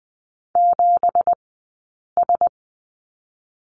Morse code, heard as a keyed tone, about 20 wpm, spaced as for 8 wpm, 700 Hz. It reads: 7S
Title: 165 Lesson 27d - Number 7 - Two Characters 20wpm